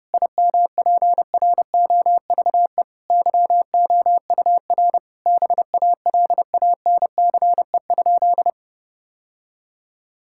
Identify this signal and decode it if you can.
Morse code, heard as a keyed tone, about 30 wpm, 700 Hz. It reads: IMPROVE YOUR BALANCE?
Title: Sentences from Top 1000 Words - Review Part 6 - 30wpm